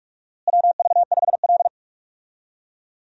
Morse code, heard as a keyed tone, about 46 wpm, 700 Hz. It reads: WV5L
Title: USA Call Signs - 4 Characters - Repeated 2x - 45wpm